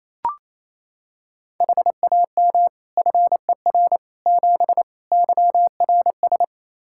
Morse code, heard as a keyed tone, about 28 wpm, 700 Hz. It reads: HAM FER 7 YRS
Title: Practice QSO - Exchange 3 - More Info - 28wpm